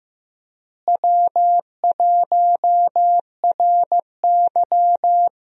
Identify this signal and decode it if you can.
Morse code, heard as a keyed tone, about 15 wpm, 700 Hz. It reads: W1RY
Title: USA Call Signs - 4 Characters - Repeated 3x - 15wpm